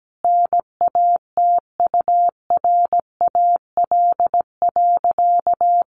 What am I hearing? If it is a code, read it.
Morse code, heard as a keyed tone, about 17 wpm, 700 Hz. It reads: NATURAL.
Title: Sentences from Top 700 Words 17wpm